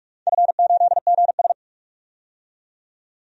Morse code, heard as a keyed tone, about 46 wpm, 700 Hz. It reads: F8GS
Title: DXCC-100 Call Signs - 4 Characters - 45wpm